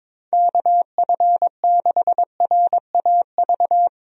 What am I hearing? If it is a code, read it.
Morse code, heard as a keyed tone, about 22 wpm, 700 Hz. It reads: KF6RAV
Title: USA Call Signs - 6 Characters - 22wpm